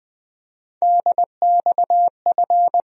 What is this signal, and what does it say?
Morse code, heard as a keyed tone, about 20 wpm, 700 Hz. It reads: DXF
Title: USA Call Sign Postfixes - 3 Letters 20wpm